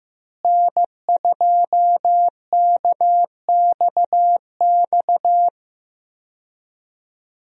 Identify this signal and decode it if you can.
Morse code, heard as a keyed tone, about 16 wpm, 700 Hz. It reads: N2KXX
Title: USA Call Signs - 5 Characters - Repeated 3x - 15wpm